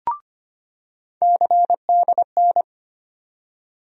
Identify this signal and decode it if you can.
Morse code, heard as a keyed tone, about 25 wpm, 700 Hz. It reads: CDN
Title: USA Call Sign Postfixes - 3 Letters 25wpm